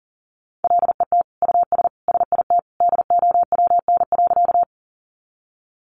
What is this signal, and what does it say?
Morse code, heard as a keyed tone, about 40 wpm, 700 Hz. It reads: LET US SIT DOWN.
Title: Sentences from Top 500 Words 40wpm